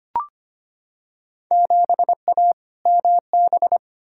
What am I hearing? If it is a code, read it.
Morse code, heard as a keyed tone, about 25 wpm, 700 Hz. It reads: 7A MB
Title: Field Day Exchanges Encore - 25wpm